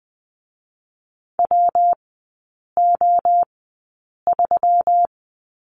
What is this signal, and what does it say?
Morse code, heard as a keyed tone, about 20 wpm, 700 Hz. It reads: W O 3
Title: 121 Lesson 20e - Number 3 - Three Characters 20wpm